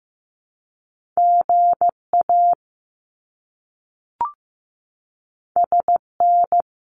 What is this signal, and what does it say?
Morse code, heard as a keyed tone, about 15 wpm, 700 Hz. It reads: GA SN
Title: Abbreviations Repeated 3x - 15wpm